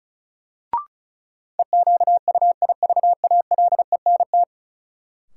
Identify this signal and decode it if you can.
Morse code, heard as a keyed tone, about 36 wpm, 700 Hz. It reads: EQUIVALENT
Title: Sentences from Top 1600 Words - New Word Drill - 35wpm